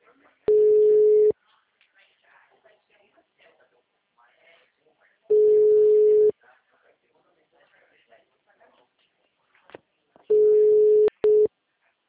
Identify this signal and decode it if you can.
Ringback tone (Europe, 425 Hz), with speech.